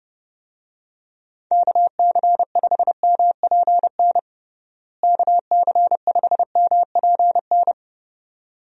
Morse code, heard as a keyed tone, about 32 wpm, 700 Hz. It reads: KC5MPN KC5MPN
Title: USA Call Signs - 6 Characters - Repeated 2x - 30wpm